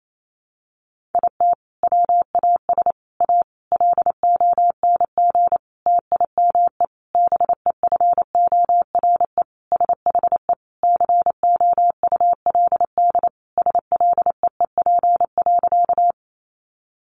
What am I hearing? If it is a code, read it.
Morse code, heard as a keyed tone, about 28 wpm, 700 Hz. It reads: IT WAS A LONG TIME BEFORE SHE COULD SLEEP.
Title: Sentences from Top 500 Words 28wpm